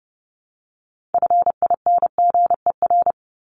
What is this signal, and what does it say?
Morse code, heard as a keyed tone, about 30 wpm, 700 Hz. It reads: FINGER
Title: Top 1501-1750 Words 30wpm